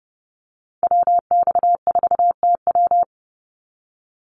Morse code, heard as a keyed tone, about 30 wpm, 700 Hz. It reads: WX4TW
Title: NA SOTA Activator Callsigns 30wpm